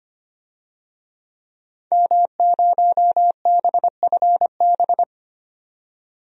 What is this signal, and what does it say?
Morse code, heard as a keyed tone, about 25 wpm, 700 Hz. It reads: M0BFB
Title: DXCC-100 Call Signs - 5 Characters - 25wpm